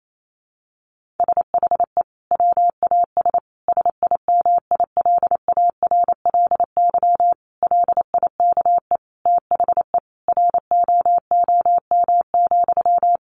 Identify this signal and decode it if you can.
Morse code, heard as a keyed tone, about 28 wpm, 700 Hz. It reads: SHE WAS SIMILARLY LIKE THE ROOM,
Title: Sentences from Top 1400 Words 28wpm